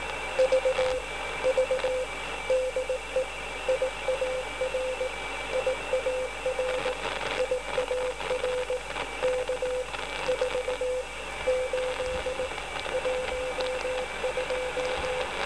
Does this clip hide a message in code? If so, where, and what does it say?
Yes: Morse code from 0.4 to 15.2 s, heard as a keyed tone, about 18 wpm, 510 Hz, with noise. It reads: V V DE IAR IAR IAR K 4 8 12